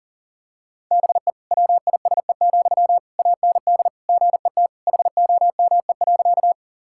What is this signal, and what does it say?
Morse code, heard as a keyed tone, about 40 wpm, 700 Hz. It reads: BE WISE, AND GET HOME.
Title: Sentences from Top 1600 Words 40wpm